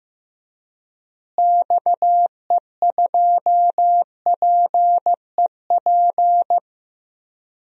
Morse code, heard as a keyed tone, about 15 wpm, 700 Hz. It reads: XE2PEP
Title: DXCC-100 Call Signs - 6 Characters - Repeated 2x - 15wpm